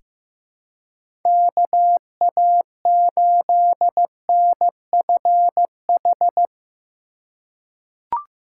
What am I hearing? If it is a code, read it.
Morse code, heard as a keyed tone, about 15 wpm, 700 Hz. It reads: KA8NFH
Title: USA Call Signs - 6 Characters - 3x Character Spacing - 15wpm